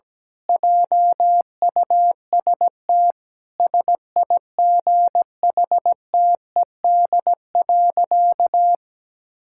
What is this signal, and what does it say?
Morse code, heard as a keyed tone, about 17 wpm, 700 Hz. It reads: JUST SIGHTED.
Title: Pangram Challenge v2 - 17wpm